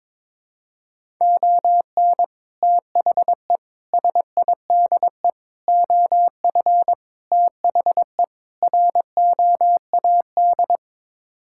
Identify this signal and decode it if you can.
Morse code, heard as a keyed tone, about 21 wpm, 700 Hz. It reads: ON THE SIDE OF THE ROAD
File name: practice / 22wpm / Sets of 6 Words - 22wpm